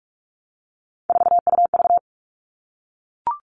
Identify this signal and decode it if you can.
Morse code, heard as a keyed tone, about 44 wpm, 700 Hz. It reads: 4UV
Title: All Cut Numbers - 3-Digit 45wpm